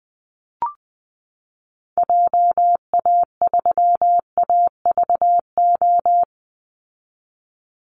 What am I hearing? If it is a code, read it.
Morse code, heard as a keyed tone, about 20 wpm, 700 Hz. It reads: JA3AVO
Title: Call Signs Repeated 3x - 20wpm